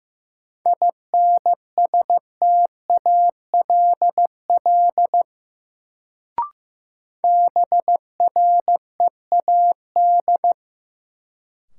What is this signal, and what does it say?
Morse code, heard as a keyed tone, about 15 wpm, 700 Hz. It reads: INSTALL BREAD
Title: Sentences from Top 1600 Words - New Word Drill Alt - 15wpm